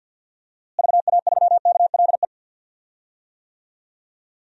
Morse code, heard as a keyed tone, about 48 wpm, 700 Hz. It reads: VA3KLE
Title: DXCC-100 Call Signs - 6 Characters - Repeated 3x - 50wpm